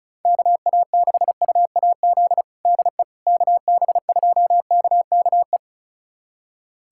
Morse code, heard as a keyed tone, about 36 wpm, 700 Hz. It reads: KA6UAZ DE KB2KKE
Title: USA Callsign DE Callsign - 6 Characters - 35wpm